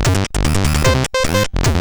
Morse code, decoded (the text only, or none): none